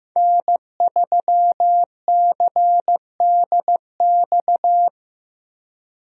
Morse code, heard as a keyed tone, about 15 wpm, 700 Hz. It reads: N3CDX
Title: USA Call Signs - 5 Characters - Repeated 2x - 15wpm